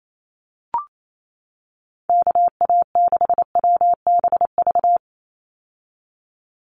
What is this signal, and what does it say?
Morse code, heard as a keyed tone, about 28 wpm, 700 Hz. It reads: KA6WBV
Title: USA Call Signs 28wpm